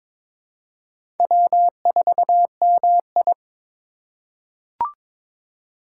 Morse code, heard as a keyed tone, about 22 wpm, 700 Hz. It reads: W4MI
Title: USA Call Signs - 4 Characters - 2x Character Spacing - 22wpm